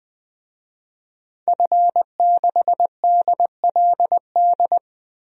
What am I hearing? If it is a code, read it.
Morse code, heard as a keyed tone, about 20 wpm, 700 Hz. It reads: F6DLD
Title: DXCC-100 Call Signs - 5 Characters - Repeated 3x - 20wpm